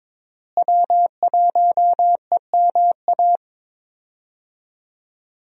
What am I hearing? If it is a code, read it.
Morse code, heard as a keyed tone, about 22 wpm, 700 Hz. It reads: W1EMA
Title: USA Call Signs - 5 Characters - Repeated 3x - 22wpm